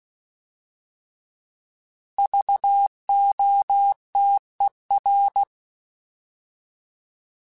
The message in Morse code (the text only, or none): VOTER